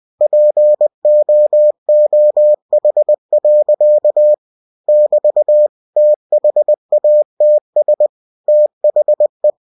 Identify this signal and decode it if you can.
Morse code, heard as a keyed tone, about 20 wpm, 600 Hz. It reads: POOH. <BT> THATS THE
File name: cw-books / winnie-the-pooh / winnie-20wpm-page-0069